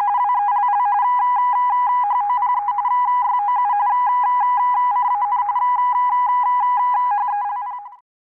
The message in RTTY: WELL HOPE YOU CAN HELP